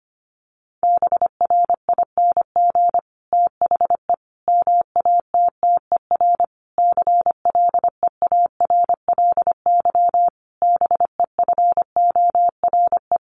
Morse code, heard as a keyed tone, about 25 wpm, 700 Hz. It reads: BRING THE MATTER CLEARLY BEFORE
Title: Sentences from Top 1300 Words 25wpm